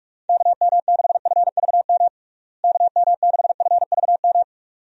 Morse code, heard as a keyed tone, about 46 wpm, 700 Hz. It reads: KM6FVM KM6FVM
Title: USA Call Signs - 6 Characters - Repeated 2x - 45wpm